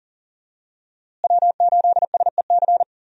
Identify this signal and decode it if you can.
Morse code, heard as a keyed tone, about 37 wpm, 700 Hz. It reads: W8SEC
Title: USA Call Signs - 5 Characters - Repeated 2x - 40wpm